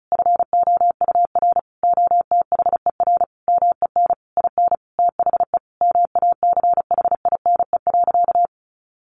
Morse code, heard as a keyed tone, about 34 wpm, 700 Hz. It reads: FOUR OTHER MEN IN THE MACHINE.
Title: A Sound Of Thunder 35wpm